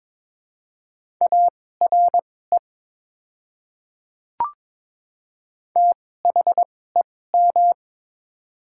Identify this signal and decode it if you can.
Morse code, heard as a keyed tone, about 22 wpm, spaced as for 16 wpm, 690 Hz. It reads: ARE THEM
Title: 2x Character Spacing - Top 100 Words - 22wpm